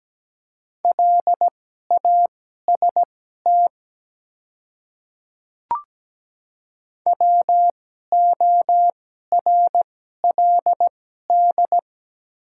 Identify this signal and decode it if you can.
Morse code, heard as a keyed tone, about 17 wpm, spaced as for 12 wpm, 700 Hz. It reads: LAST WORLD
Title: 2x Character Spacing - Top 300 Words - 17wpm